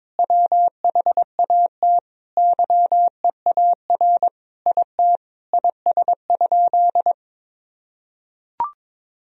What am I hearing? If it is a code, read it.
Morse code, heard as a keyed tone, about 22 wpm, 700 Hz. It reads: WHAT YEAR IT IS?
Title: Sentences from Top 1000 Words - Review Part 1 - 22wpm